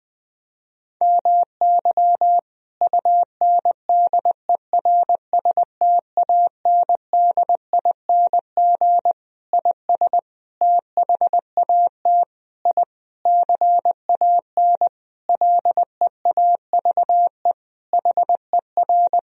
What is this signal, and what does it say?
Morse code, heard as a keyed tone, about 20 wpm, 700 Hz. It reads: MY UNDERSTANDING IS THAT I CAN LEAVE HER
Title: Sentences from Top 1000 Words - Review Part 3 - 20wpm